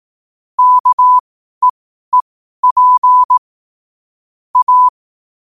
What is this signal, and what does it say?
Morse code, heard as a keyed tone, about 18 wpm, spaced as for 12 wpm, 1000 Hz. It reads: KEEP A
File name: qotdfiles / 12wpm